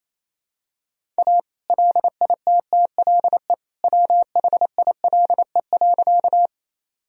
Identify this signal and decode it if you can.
Morse code, heard as a keyed tone, about 28 wpm, 700 Hz. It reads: A LITTLE WHILE.
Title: Sentences from Top 1000 Words - Review Part 1 - 28wpm